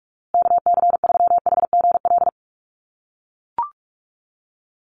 Morse code, heard as a keyed tone, about 44 wpm, 700 Hz. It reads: KC3HGL
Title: USA Call Signs Repeated 2x 45wpm